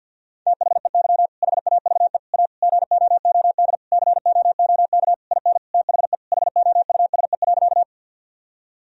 Morse code, heard as a keyed tone, about 50 wpm, 700 Hz. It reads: THEY HAVE A GOOD COOK IN THE HOUSE.
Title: Sentences from Top 1000 Words - Review Part 4 - 50wpm